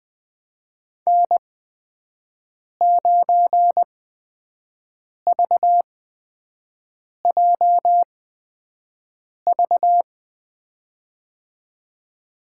Morse code, heard as a keyed tone, about 20 wpm, spaced as for 5 wpm, 700 Hz. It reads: N9VJV
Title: USA Call Signs - 5 Characters - 8x Character Spacing - 20wpm